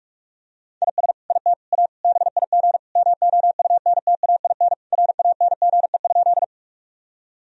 Morse code, heard as a keyed tone, about 41 wpm, 700 Hz. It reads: IS IT A BIG MOUNTAIN RANGE?
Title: Sentences from Top 1000 Words - Review Part 4 - 45wpm